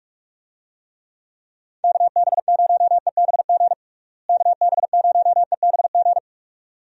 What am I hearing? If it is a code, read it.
Morse code, heard as a keyed tone, about 43 wpm, 700 Hz. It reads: KB0EBG KB0EBG
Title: USA Call Signs - 6 Characters - Repeated 2x - 45wpm